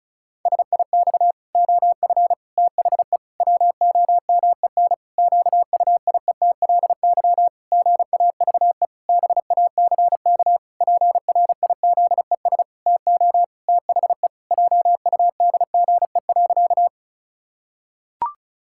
Morse code, heard as a keyed tone, about 36 wpm, 700 Hz. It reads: SIX OF THE WOMEN QUIETLY GAVE BACK PRIZES TO THE JUDGE.
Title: Pangram Challenge v2 - 35wpm